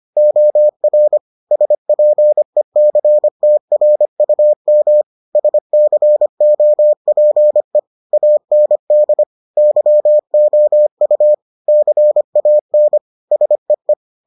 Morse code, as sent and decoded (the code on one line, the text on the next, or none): --- .-. / ... .--. -.-. - .-. ..- -- / ... -.-. --- .--. / .- -. -.. / -.-- --- ..- / -.-. .- -. / ... .
OR SPECTRUM SCOPE AND YOU CAN SEE